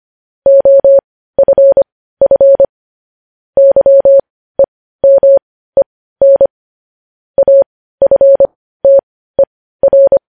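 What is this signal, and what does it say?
Morse code, heard as a keyed tone, about 25 wpm, spaced as for 15 wpm, 550 Hz. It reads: OFF YEMEN AFTER
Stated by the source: Morse news headlines sent at 15 WPM, beginning with vvv.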